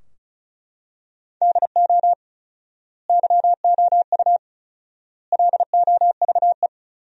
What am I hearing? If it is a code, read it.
Morse code, heard as a keyed tone, about 36 wpm, 700 Hz. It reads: DO YOU LOVE
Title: 4x Word-Spacing - Sentences from Top 400 Words - 35wpm